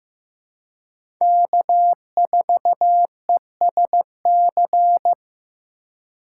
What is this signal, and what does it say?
Morse code, heard as a keyed tone, about 15 wpm, 700 Hz. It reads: K4ESC
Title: USA Call Signs - 5 Characters - 3x Character Spacing - 15wpm